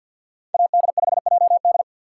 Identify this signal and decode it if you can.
Morse code, heard as a keyed tone, about 52 wpm, 690 Hz. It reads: AD5JD
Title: USA Call Signs - 4x Character Spacing - 50wpm